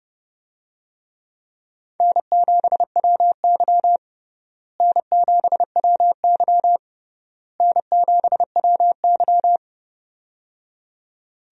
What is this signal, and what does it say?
Morse code, heard as a keyed tone, about 30 wpm, 700 Hz. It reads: N7WY N7WY N7WY
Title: USA Call Signs - 4 Characters - Repeated 3x - 30wpm